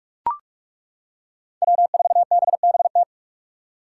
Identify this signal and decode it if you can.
Morse code, heard as a keyed tone, about 42 wpm, 700 Hz. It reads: W4BBT
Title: K1USN SST QSO Practice - 45wpm